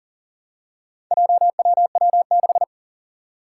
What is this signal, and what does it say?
Morse code, heard as a keyed tone, about 40 wpm, 700 Hz. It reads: JWW6
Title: Four Challenging Letter-Number Combinations 40wpm